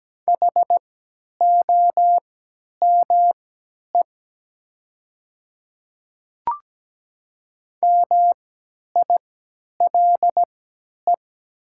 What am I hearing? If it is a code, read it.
Morse code, heard as a keyed tone, about 17 wpm, spaced as for 10 wpm, 700 Hz. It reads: HOME MILE
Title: 3x Character Spacing - Top 300 Words - 17wpm